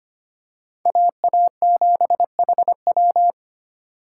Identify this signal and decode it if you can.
Morse code, heard as a keyed tone, about 25 wpm, 700 Hz. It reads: AA7HW
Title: USA Call Signs - 5 Characters - 2x Character Spacing - 25wpm